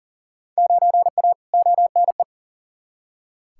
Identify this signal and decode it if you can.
Morse code, heard as a keyed tone, about 40 wpm, 700 Hz. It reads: 9A ONE
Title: Field Day Exchanges Encore - 40wpm